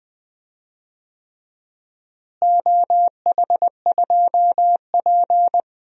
Morse code, heard as a keyed tone, about 20 wpm, 700 Hz. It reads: OH2P
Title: DXCC-100 Call Signs - 4 Characters - Repeated 3x - 20wpm